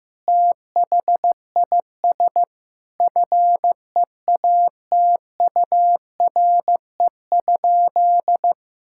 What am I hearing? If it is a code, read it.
Morse code, heard as a keyed tone, about 15 wpm, 700 Hz. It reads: THIS FEATURE?
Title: Sentences from Top 1000 Words - Review Part 5 - 15wpm